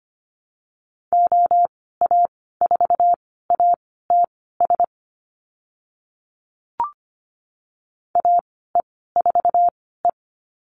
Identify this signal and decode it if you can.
Morse code, heard as a keyed tone, about 25 wpm, spaced as for 16 wpm, 700 Hz. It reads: OA4ATS AE4E
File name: Lesson 15g - Number 5 - Call Signs 25wpm